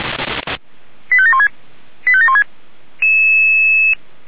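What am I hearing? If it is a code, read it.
Five-tone selective call (ZVEI), heard as quick runs of five tones, tones of about 80 ms. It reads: 85716 85716
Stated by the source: ZVEI five-tone selective call 89291 received by SDR.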